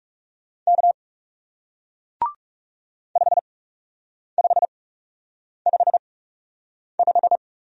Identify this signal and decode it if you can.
Morse code, heard as a keyed tone, about 38 wpm, spaced as for 8 wpm, 700 Hz. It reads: K 5555